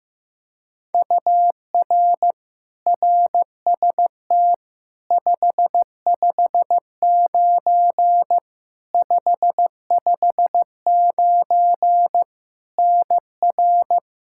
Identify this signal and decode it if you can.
Morse code, heard as a keyed tone, about 15 wpm, 700 Hz. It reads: UR RST 559 559 NR
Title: Practice QSO - Exchange 1 - Beginning - 15wpm